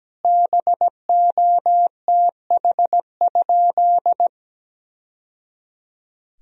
Morse code, heard as a keyed tone, about 17 wpm, 700 Hz. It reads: BOTH?
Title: Sentences from Top 1000 Words - Review Part 1 - 17wpm